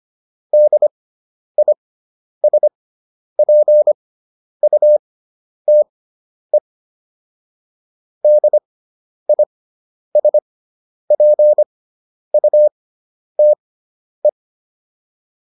Morse code, heard as a keyed tone, about 25 wpm, spaced as for 10 wpm, 600 Hz. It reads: DISPUTE DISPUTE